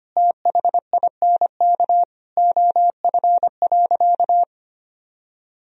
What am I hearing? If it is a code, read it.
Morse code, heard as a keyed tone, about 25 wpm, 700 Hz. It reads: THINK OF.
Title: Sentences from Top 1000 Words - Review Part 1 - 25wpm